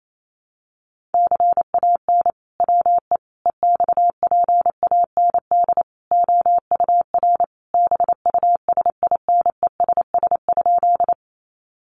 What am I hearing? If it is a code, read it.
Morse code, heard as a keyed tone, about 28 wpm, 700 Hz. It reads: CAN WE EXPAND OUR BUSINESS?
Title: Sentences from Top 1400 Words 28wpm